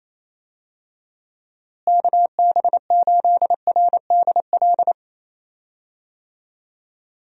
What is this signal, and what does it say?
Morse code, heard as a keyed tone, about 27 wpm, 700 Hz. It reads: KB8RDL
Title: USA Call Signs - 6 Characters - Repeated 2x - 28wpm